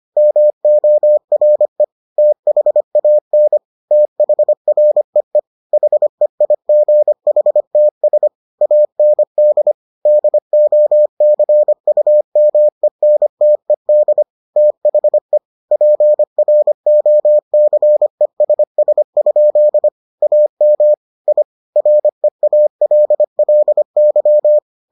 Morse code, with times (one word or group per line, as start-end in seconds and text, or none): 0.2-1.8 MORE
2.2-3.6 THAN
3.9-5.4 THREE
5.7-8.3 HEIGHTS
8.6-9.7 AND
10.1-14.2 DOCUMENTED
14.6-15.4 THE
15.7-19.9 PROCESS?
20.2-20.9 AM
21.3-21.4 I
21.8-24.6 REALLY